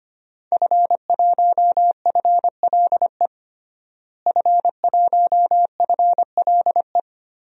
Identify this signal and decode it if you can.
Morse code, heard as a keyed tone, about 25 wpm, 700 Hz. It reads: F1FLE F1FLE
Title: DXCC-100 Call Signs - 5 Characters - Repeated 2x - 25wpm